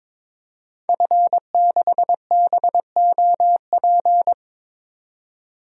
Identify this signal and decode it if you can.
Morse code, heard as a keyed tone, about 21 wpm, 700 Hz. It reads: F6BOP
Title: DXCC-100 Call Signs - 5 Characters - 22wpm